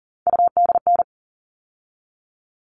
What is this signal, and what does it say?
Morse code, heard as a keyed tone, about 40 wpm, 700 Hz. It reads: UDN
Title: All Cut Numbers - 3-Digit 40wpm